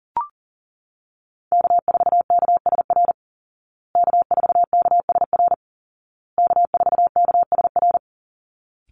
Morse code, heard as a keyed tone, about 40 wpm, 700 Hz. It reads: K4KSR K4KSR K4KSR
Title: Call Signs Repeated 2x - 40wpm